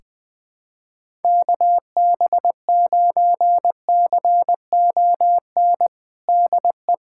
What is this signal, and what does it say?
Morse code, heard as a keyed tone, about 19 wpm, 700 Hz. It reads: KB9CON DE
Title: USA Callsign DE Callsign 20wpm